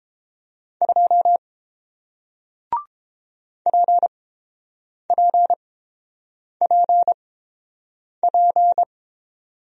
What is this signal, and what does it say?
Morse code, heard as a keyed tone, about 30 wpm, spaced as for 7 wpm, 700 Hz. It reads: 2 PPPP